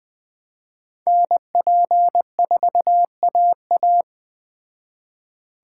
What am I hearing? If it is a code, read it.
Morse code, heard as a keyed tone, about 21 wpm, 700 Hz. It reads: NP4AA
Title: Speed Racing - USA 2x2 Call Signs 17wpm